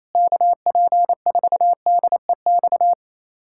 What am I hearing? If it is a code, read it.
Morse code, heard as a keyed tone, about 28 wpm, 700 Hz. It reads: KP4DEX
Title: USA Callsign DE Callsign - 6 Characters - 28wpm